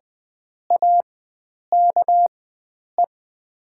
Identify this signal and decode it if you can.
Morse code, heard as a keyed tone, about 20 wpm, spaced as for 9 wpm, 690 Hz. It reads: AKE
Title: 4x Character Spacing - Top 500 Words - 20wpm